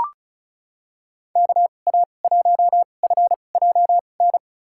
Morse code, heard as a keyed tone, about 36 wpm, spaced as for 25 wpm, 700 Hz. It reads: KA1FJN
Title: USA Call Signs - 6 Characters - 2x Character Spacing - 35wpm